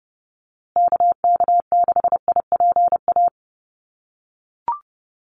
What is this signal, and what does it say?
Morse code, heard as a keyed tone, about 30 wpm, 700 Hz. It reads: KK6IPA
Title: USA Call Signs Repeated 2x 30wpm